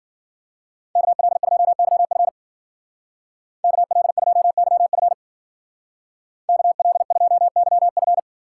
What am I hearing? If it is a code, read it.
Morse code, heard as a keyed tone, about 50 wpm, 700 Hz. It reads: KL2YF KL2YF KL2YF
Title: Speed Racing - USA 2x2 Call Signs 40wpm